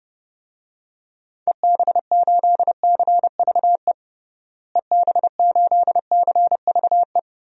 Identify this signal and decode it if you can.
Morse code, heard as a keyed tone, about 30 wpm, 700 Hz. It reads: EB8CVE EB8CVE
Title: DXCC-100 Call Signs - 6 Characters - Repeated 3x - 30wpm